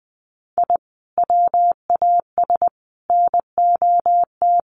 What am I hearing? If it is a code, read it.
Morse code, heard as a keyed tone, about 20 wpm, 700 Hz. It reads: I WAS NOT
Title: Sets of 3 Words 20wpm